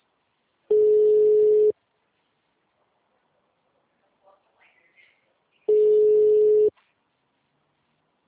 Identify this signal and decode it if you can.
Ringback tone (Europe, 425 Hz).